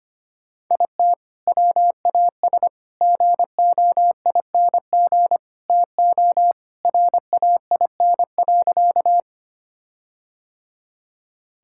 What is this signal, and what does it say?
Morse code, heard as a keyed tone, about 25 wpm, 700 Hz. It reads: IT WAS GOING TO RAIN.
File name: practice / 25wpm / Sentences with 4-Word Phrases - 25wpm